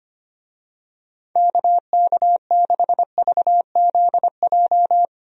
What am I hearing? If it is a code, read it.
Morse code, heard as a keyed tone, about 25 wpm, 700 Hz. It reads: KK6VZJ
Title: USA Call Signs - 6 Characters - Repeated 2x - 25wpm